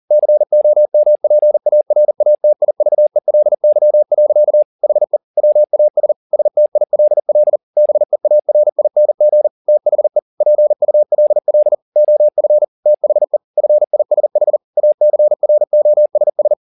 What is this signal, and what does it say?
Morse code, heard as a keyed tone, about 40 wpm, 600 Hz. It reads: COMPARATIVELY. HE WAS STILL BEARING THE PULL OF THE FISH ACROSS
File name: the old man and the sea 2 40wpm